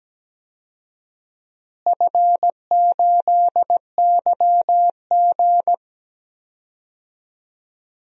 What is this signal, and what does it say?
Morse code, heard as a keyed tone, about 17 wpm, 700 Hz. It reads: F8YG
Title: DXCC-100 Call Signs - 4 Characters - Repeated 2x - 17wpm